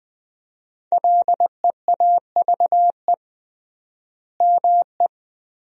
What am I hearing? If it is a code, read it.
Morse code, heard as a keyed tone, about 21 wpm, 700 Hz. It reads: LEAVE ME
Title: 3x Word-Spacing - Sentences from Top 400 Words - 20wpm